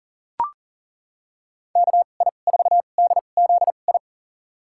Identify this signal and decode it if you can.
Morse code, heard as a keyed tone, about 40 wpm, spaced as for 29 wpm, 700 Hz. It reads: KI4DZI
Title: USA Call Signs - 6 Characters - 2x Character Spacing - 40wpm